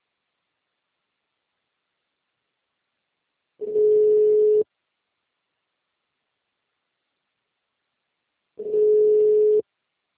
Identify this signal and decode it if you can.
Ringback tone (Europe, 425 Hz).